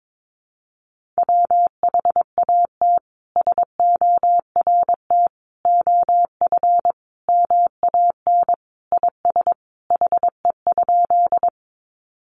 Morse code, heard as a keyed tone, about 22 wpm, 700 Hz. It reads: WHAT SORT OF MAN IS HE?
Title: Sentences from Top 1300 Words 22wpm